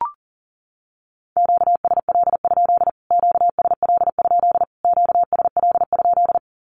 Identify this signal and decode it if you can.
Morse code, heard as a keyed tone, about 40 wpm, 700 Hz. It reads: QSL? QSL? QSL?
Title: Q-Codes Repeated 3x 40wpm